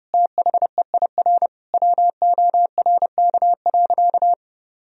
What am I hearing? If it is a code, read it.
Morse code, heard as a keyed tone, about 32 wpm, 700 Hz. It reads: THEIR WORK.
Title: Sentences from Top 1000 Words - Review Part 3 - 30wpm